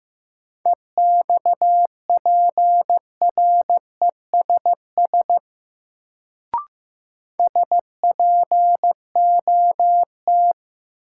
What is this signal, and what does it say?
Morse code, heard as a keyed tone, about 15 wpm, 700 Hz. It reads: EXPRESS SPOT
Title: Sentences from Top 1100 Words - New Word Drill - 15wpm